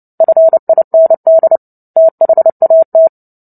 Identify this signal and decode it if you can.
Morse code, heard as a keyed tone, about 30 wpm, 650 Hz. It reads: FIND THAT
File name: Book Two / 03-Chapter 3 - 29 WPM